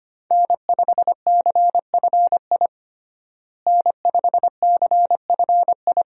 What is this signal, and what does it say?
Morse code, heard as a keyed tone, about 25 wpm, 700 Hz. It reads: N5CFI N5CFI
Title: USA Call Signs - 5 Characters - Repeated 3x - 25wpm